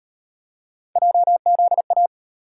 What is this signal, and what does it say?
Morse code, heard as a keyed tone, about 35 wpm, 700 Hz. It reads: JZA